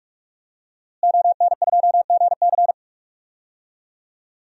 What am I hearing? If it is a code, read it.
Morse code, heard as a keyed tone, about 46 wpm, 700 Hz. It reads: ON2GC
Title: DXCC-100 Call Signs - 5 Characters - Repeated 3x - 45wpm